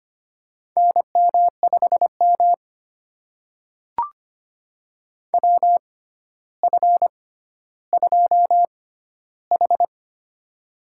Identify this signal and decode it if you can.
Morse code, heard as a keyed tone, about 25 wpm, spaced as for 13 wpm, 700 Hz. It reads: NM5M WF2H
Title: USA Call Signs - 4 Characters - 6x Character Spacing - 25wpm